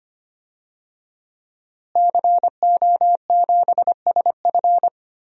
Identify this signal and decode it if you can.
Morse code, heard as a keyed tone, about 25 wpm, 700 Hz. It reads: CO7SF
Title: DXCC-100 Call Signs - 5 Characters - Repeated 2x - 25wpm